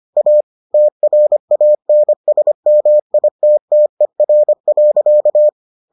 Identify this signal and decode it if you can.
Morse code, heard as a keyed tone, about 25 wpm, 600 Hz. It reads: A TRANSMITTER.